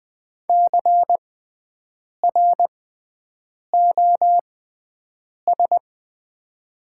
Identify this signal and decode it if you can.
Morse code, heard as a keyed tone, about 20 wpm, spaced as for 7 wpm, 700 Hz. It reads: CROS
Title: 6x Character Spacing - Top 300 Words - 20wpm